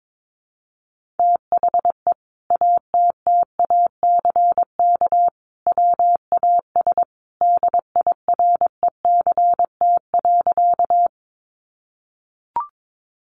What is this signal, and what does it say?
Morse code, heard as a keyed tone, about 22 wpm, 700 Hz. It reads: THE ATTACK WAS DIRECT.
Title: Sentences from Top 1500 Words 22wpm